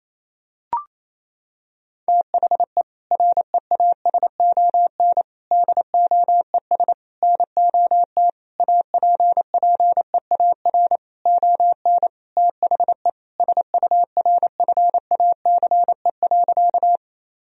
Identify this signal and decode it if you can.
Morse code, heard as a keyed tone, about 28 wpm, 700 Hz. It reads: THE REASON DOES NOT APPEAR ON THE SURFACE.
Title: Sentences from Top 1000 Words - Review Part 3 - 28wpm